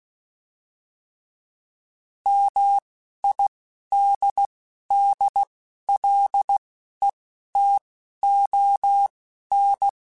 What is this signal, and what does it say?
Morse code, heard as a keyed tone, about 16 wpm, spaced as for 11 wpm, 800 Hz. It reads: MIDDLETON
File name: fromSMs / FromSMs mp3s / Cities@10